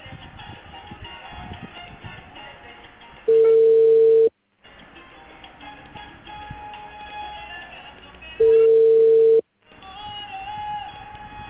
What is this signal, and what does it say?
Ringback tone (China, 450 Hz).